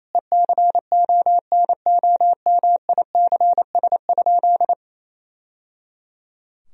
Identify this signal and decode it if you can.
Morse code, heard as a keyed tone, about 28 wpm, 700 Hz. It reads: ECONOMICS?
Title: Sentences from Top 1000 Words - Review Part 3 - 28wpm